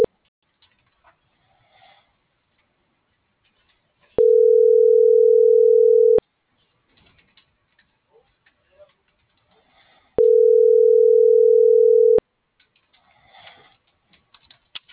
Ringback tone (North America), 440 + 480 Hz.